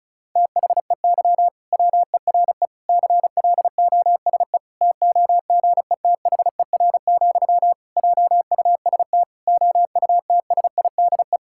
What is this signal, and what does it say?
Morse code, heard as a keyed tone, about 36 wpm, 700 Hz. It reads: THEY WERE CLOSE TOGETHER, JUST OUTSIDE
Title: Sentences from Top 1000 Words - Review Part 6 - 35wpm